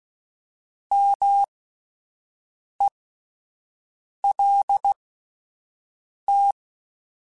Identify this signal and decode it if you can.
Morse code, heard as a keyed tone, about 16 wpm, spaced as for 5 wpm, 800 Hz. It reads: MELT
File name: Alphabetical / Alphabetical mp3s / Ms @5